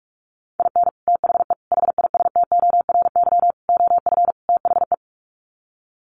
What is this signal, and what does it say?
Morse code, heard as a keyed tone, about 44 wpm, 700 Hz. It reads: IN THE HISTORY OF THE
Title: Sets of 5 Words 45wpm